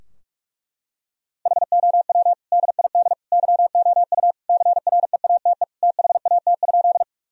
Morse code, heard as a keyed tone, about 42 wpm, 700 Hz. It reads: HOW DID YOU CREATE THAT?
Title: Sentences from Top 1000 Words - Review Part 3 - 45wpm